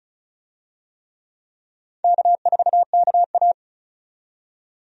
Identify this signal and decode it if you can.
Morse code, heard as a keyed tone, about 36 wpm, 700 Hz. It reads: K4KA
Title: USA Call Signs - 4 Characters - 35wpm